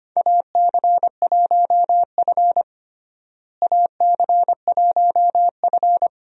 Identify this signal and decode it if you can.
Morse code, heard as a keyed tone, about 25 wpm, 700 Hz. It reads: AC1F AC1F
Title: USA Call Signs - 4 Characters - Repeated 2x - 25wpm